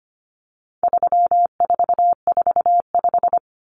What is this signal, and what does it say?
Morse code, heard as a keyed tone, about 25 wpm, 700 Hz. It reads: 3445
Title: Numbers - 4-Digit 25wpm